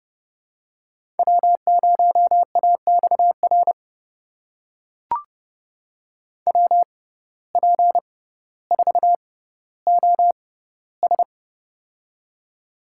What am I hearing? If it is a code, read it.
Morse code, heard as a keyed tone, about 30 wpm, spaced as for 15 wpm, 700 Hz. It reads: W0AXR WP4OS
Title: USA Call Signs - 5 Characters - 6x Character Spacing - 30wpm